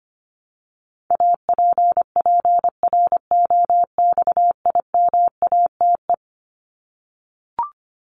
Morse code, heard as a keyed tone, about 25 wpm, 700 Hz. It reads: APPROXIMATE
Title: LY Suffix - Top 100 Words with Base Words 25wpm